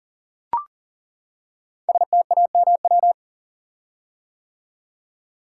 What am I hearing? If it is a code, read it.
Morse code, heard as a keyed tone, about 40 wpm, 700 Hz. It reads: STAMW
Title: Code Groups - 5 Letters - 40wpm